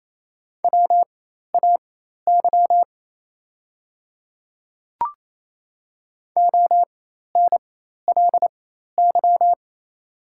Morse code, heard as a keyed tone, about 28 wpm, spaced as for 13 wpm, 690 Hz. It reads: WAY ONLY
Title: 4x Character Spacing - Top 200 Words - 28wpm